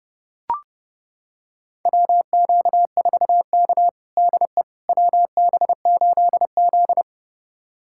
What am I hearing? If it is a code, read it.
Morse code, heard as a keyed tone, about 30 wpm, 700 Hz. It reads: WQ4K DE WB8Z
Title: USA Callsign DE Callsign - 4 Characters - 30wpm